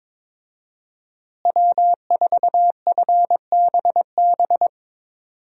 Morse code, heard as a keyed tone, about 22 wpm, 700 Hz. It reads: W4FBB
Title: USA Call Signs - 5 Characters - Repeated 2x - 22wpm